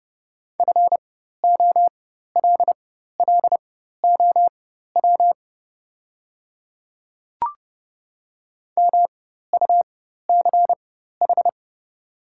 Morse code, heard as a keyed tone, about 30 wpm, spaced as for 14 wpm, 700 Hz. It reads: FOLLOW MUCH
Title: 4x Character Spacing - Top 200 Words - 30wpm